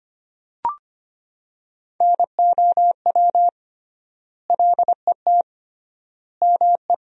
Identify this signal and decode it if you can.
Morse code, heard as a keyed tone, about 25 wpm, 700 Hz. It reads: NOW LET ME
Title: 3x Word-Spacing - Sentences from Top 300 Words - 25wpm